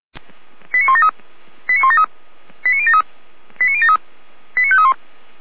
Five-tone selective call (ZVEI), heard as quick runs of five tones, tones of about 70 ms. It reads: 89173 89173 89073 89073 89531